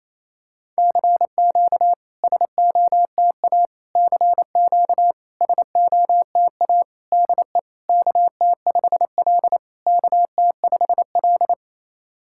Morse code, heard as a keyed tone, about 28 wpm, 690 Hz. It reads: CQ SOTA CQ SOTA DE KT5L KT5L
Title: SOTA QSO Practice 28wpm